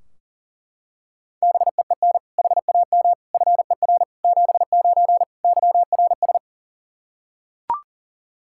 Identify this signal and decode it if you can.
Morse code, heard as a keyed tone, about 40 wpm, 700 Hz. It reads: BEEN HAM FER 79 YRS
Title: Practice QSO - Exchange 3 - More Info - 40wpm